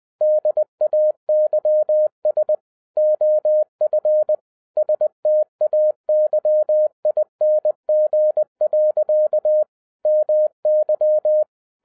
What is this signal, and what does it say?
Morse code, heard as a keyed tone, about 20 wpm, 610 Hz. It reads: DAYS OF STAYING. MY
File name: 2011-07-mp3 / n58 BACK FROM A BIG ACCIDENT 20wpm